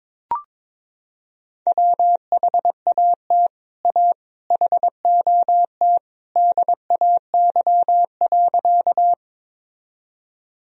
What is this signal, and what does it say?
Morse code, heard as a keyed tone, about 22 wpm, 700 Hz. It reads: WHAT A HOT DAY.
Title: Sentences from Top 1000 Words - Review Part 3 - 22wpm